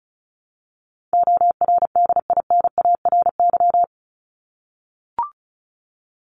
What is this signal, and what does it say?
Morse code, heard as a keyed tone, about 34 wpm, 700 Hz. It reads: ORDINARY
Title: Top 1751-2000 Words 35wpm